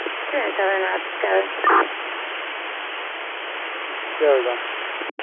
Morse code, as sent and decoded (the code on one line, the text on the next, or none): none